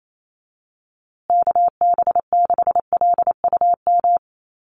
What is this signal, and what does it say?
Morse code, heard as a keyed tone, about 28 wpm, 700 Hz. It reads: KB6LUM
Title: USA Call Signs Repeated 2x 28wpm